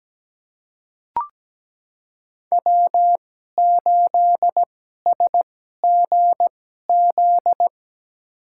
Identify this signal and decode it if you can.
Morse code, heard as a keyed tone, about 17 wpm, spaced as for 12 wpm, 700 Hz. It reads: W8SGZ
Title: USA Call Signs - 5 Characters - 2x Character Spacing - 17wpm